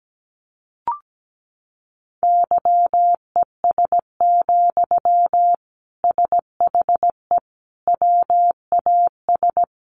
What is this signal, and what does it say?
Morse code, heard as a keyed tone, about 17 wpm, 700 Hz. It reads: YES, SHE WAS
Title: Sentences from Top 1100 Words 17wpm